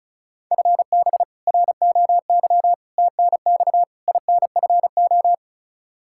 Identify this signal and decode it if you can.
Morse code, heard as a keyed tone, about 36 wpm, 700 Hz. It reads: FB ROY TNX INFO
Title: K1USN SST QSO Practice 35wpm